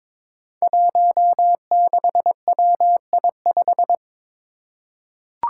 Morse code, heard as a keyed tone, about 22 wpm, 700 Hz. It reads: 16WI5
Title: Five Challenging Letter-Number Combinations 22wpm